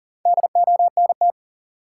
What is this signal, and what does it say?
Morse code, heard as a keyed tone, about 40 wpm, 700 Hz. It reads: DONT
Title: Sentences from Top 300 Words - New Word Drill - 40wpm